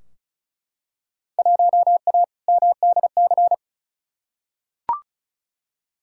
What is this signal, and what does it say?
Morse code, heard as a keyed tone, about 36 wpm, 700 Hz. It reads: 1A MDC
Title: Field Day Exchanges - 35wpm